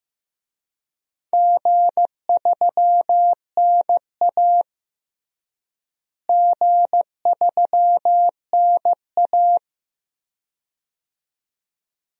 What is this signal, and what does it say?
Morse code, heard as a keyed tone, about 15 wpm, 700 Hz. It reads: G3NA G3NA
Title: DXCC-100 Call Signs - 4 Characters - Repeated 2x - 15wpm